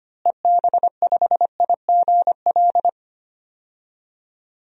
Morse code, heard as a keyed tone, about 25 wpm, 700 Hz. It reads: EB5IGL
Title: DXCC-100 Call Signs - 6 Characters - Repeated 3x - 25wpm